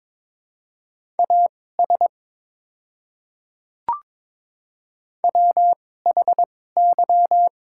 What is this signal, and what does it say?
Morse code, heard as a keyed tone, about 22 wpm, spaced as for 16 wpm, 700 Hz. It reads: AS WHY
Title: 2x Character Spacing - Top 300 Words - 22wpm